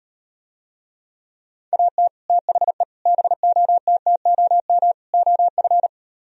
Morse code, heard as a keyed tone, about 38 wpm, 700 Hz. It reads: AT THE BOTTOM OF